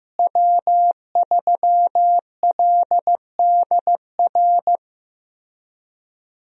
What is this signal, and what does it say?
Morse code, heard as a keyed tone, about 15 wpm, 700 Hz. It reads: W3LDR
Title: USA Call Signs - 5 Characters - Repeated 3x - 15wpm